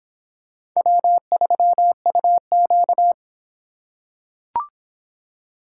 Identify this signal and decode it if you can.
Morse code, heard as a keyed tone, about 24 wpm, 700 Hz. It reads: W3UQ